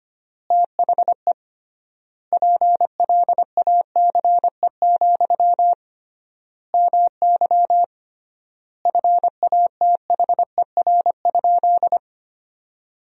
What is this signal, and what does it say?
Morse code, heard as a keyed tone, about 25 wpm, 700 Hz. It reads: THE PLACE, MY FATHER?
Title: 3x Word-Spacing - Sentences from Top 400 Words - 25wpm